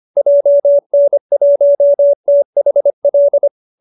Morse code, heard as a keyed tone, about 25 wpm, 570 Hz. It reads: JN1THL